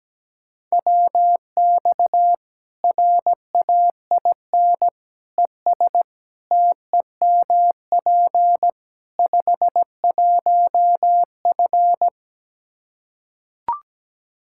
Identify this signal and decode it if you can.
Morse code, heard as a keyed tone, about 17 wpm, 700 Hz. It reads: WX RAIN ES TEMP 51F
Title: Practice QSO - Exchange 2 - Info - 17wpm